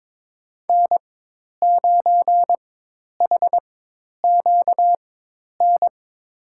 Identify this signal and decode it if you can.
Morse code, heard as a keyed tone, about 22 wpm, spaced as for 10 wpm, 700 Hz. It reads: N9HQN
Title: USA Call Signs - 5 Characters - 4x Character Spacing - 22wpm